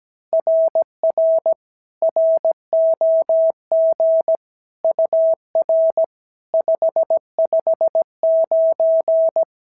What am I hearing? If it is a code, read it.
Morse code, heard as a keyed tone, about 16 wpm, 650 Hz. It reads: RR ROG UR 559
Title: SOTA QSO Practice 17wpm